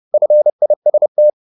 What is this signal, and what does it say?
Morse code, heard as a keyed tone, about 30 wpm, 600 Hz. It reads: FIST